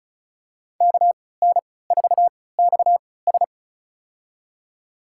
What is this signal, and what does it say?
Morse code, heard as a keyed tone, about 34 wpm, spaced as for 20 wpm, 700 Hz. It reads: KN4XS
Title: USA Call Signs - 3x Character Spacing - 35wpm